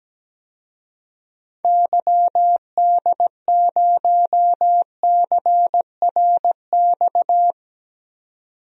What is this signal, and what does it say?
Morse code, heard as a keyed tone, about 17 wpm, 700 Hz. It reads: YD0CRX
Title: DXCC-100 Call Signs - 6 Characters - 17wpm